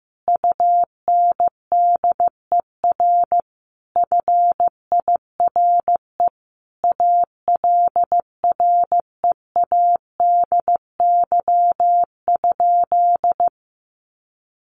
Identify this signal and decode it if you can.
Morse code, heard as a keyed tone, about 15 wpm, 700 Hz. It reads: UNDER FIRE ALREADY?
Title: Sentences from Top 700 Words 15wpm